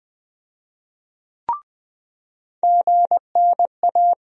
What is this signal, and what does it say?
Morse code, heard as a keyed tone, about 20 wpm, 700 Hz. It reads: GNA
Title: USA Call Sign Postfixes - 3 Letters 20wpm